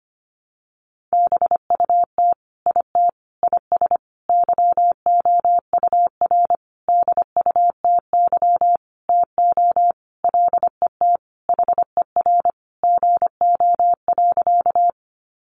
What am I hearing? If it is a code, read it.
Morse code, heard as a keyed tone, about 25 wpm, 700 Hz. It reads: BUT IT IS YOUR DUTY TO LET HER GO.
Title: Sentences from Top 1500 Words 25wpm